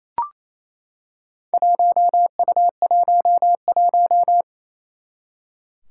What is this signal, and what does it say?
Morse code, heard as a keyed tone, about 28 wpm, 700 Hz. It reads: 1U11
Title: Four Challenging Letter-Number Combinations 28wpm